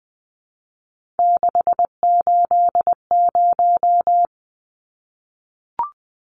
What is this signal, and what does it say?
Morse code, heard as a keyed tone, about 20 wpm, 700 Hz. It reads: 680
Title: Numbers - 3-Digit 20wpm